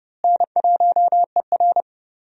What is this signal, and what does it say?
Morse code, heard as a keyed tone, about 30 wpm, 700 Hz. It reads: N1ER
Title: USA Call Signs - 4 Characters - Repeated 3x - 30wpm